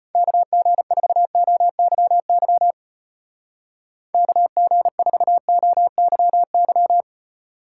Morse code, heard as a keyed tone, about 31 wpm, 700 Hz. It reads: KG4OYY KG4OYY